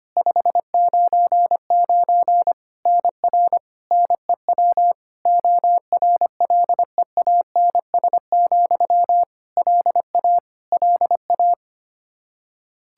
Morse code, heard as a keyed tone, about 25 wpm, 700 Hz. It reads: 599 NR NEW ORLEANS, LA LA
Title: Practice QSO - Exchange 1 - Beginning - 25wpm